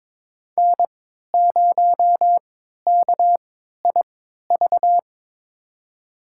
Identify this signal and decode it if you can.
Morse code, heard as a keyed tone, about 22 wpm, spaced as for 12 wpm, 700 Hz. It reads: N0KIV
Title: USA Call Signs - 5 Characters - 3x Character Spacing - 22wpm